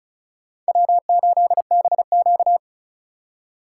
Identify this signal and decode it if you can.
Morse code, heard as a keyed tone, about 36 wpm, 700 Hz. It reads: W8BQ
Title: USA Call Signs - 4 Characters - 2x Character Spacing - 35wpm